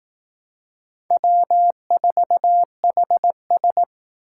Morse code, heard as a keyed tone, about 18 wpm, 700 Hz. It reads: W4HS
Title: Speed Racing - USA Call Signs 15wpm